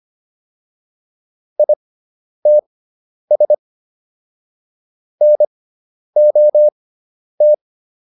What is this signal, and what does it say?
Morse code, heard as a keyed tone, about 25 wpm, spaced as for 10 wpm, 600 Hz. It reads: ITS NOT